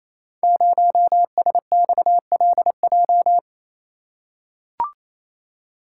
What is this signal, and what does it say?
Morse code, heard as a keyed tone, about 27 wpm, 700 Hz. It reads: 0SXLJ
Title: Code Groups - 5 Letter:Numbers - 28wpm